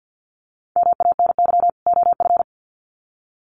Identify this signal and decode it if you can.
Morse code, heard as a keyed tone, about 50 wpm, 700 Hz. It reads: MANY OF
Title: Sets of 2 Words 50wpm